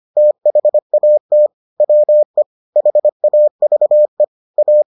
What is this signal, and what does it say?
Morse code, heard as a keyed tone, about 25 wpm, 600 Hz. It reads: THAT WE HAVE A